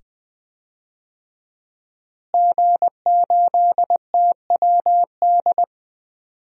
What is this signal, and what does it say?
Morse code, heard as a keyed tone, about 21 wpm, 700 Hz. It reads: G8TWD
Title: DXCC-100 Call Signs - 5 Characters - Repeated 2x - 20wpm